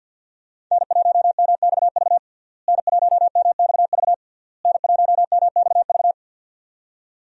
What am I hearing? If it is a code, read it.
Morse code, heard as a keyed tone, about 48 wpm, 700 Hz. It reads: N1MXV N1MXV N1MXV
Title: USA Call Signs - 5 Characters - Repeated 3x - 50wpm